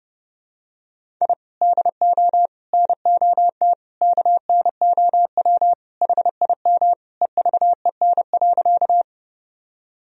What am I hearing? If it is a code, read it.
Morse code, heard as a keyed tone, about 30 wpm, 700 Hz. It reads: I DO NOT KNOW HIM EVEN.
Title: Sentences from Top 1000 Words - Review Part 1 - 30wpm